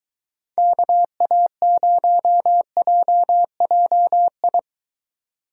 Morse code, heard as a keyed tone, about 23 wpm, 700 Hz. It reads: KA0JJI